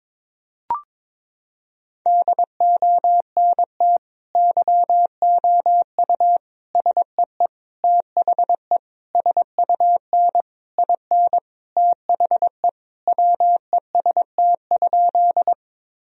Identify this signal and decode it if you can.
Morse code, heard as a keyed tone, about 22 wpm, 700 Hz. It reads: DONT YOU SEE THE SUN IN THE WEST?
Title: Sentences from Top 1000 Words - Review Part 2 - 22wpm